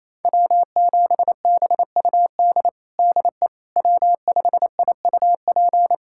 Morse code, heard as a keyed tone, about 27 wpm, 700 Hz. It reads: W7BUD DE W5IUP
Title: USA Callsign DE Callsign - 5 Characters - 28wpm